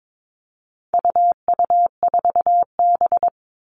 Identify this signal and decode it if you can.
Morse code, heard as a keyed tone, about 22 wpm, 700 Hz. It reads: UU4B
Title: All Cut Numbers - 4-Digit 22wpm